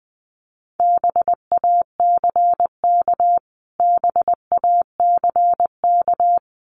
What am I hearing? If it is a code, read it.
Morse code, heard as a keyed tone, about 20 wpm, 700 Hz. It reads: BACK BACK
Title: Top 300 Words Repeated 20wpm